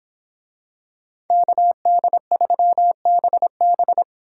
Morse code, heard as a keyed tone, about 26 wpm, 700 Hz. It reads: KD3BB